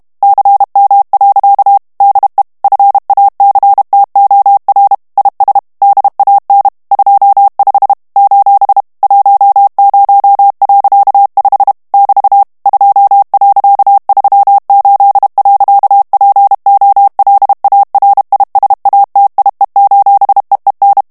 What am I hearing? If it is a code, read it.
Morse code, heard as a keyed tone, about 32 wpm, 800 Hz. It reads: CM. DE FACTOR IS DAN 25 : 10.5 <BT> 2.38. POLARISATIE: EEN
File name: HOLLANDAIS texte clair 1 vitesse 30 mots minute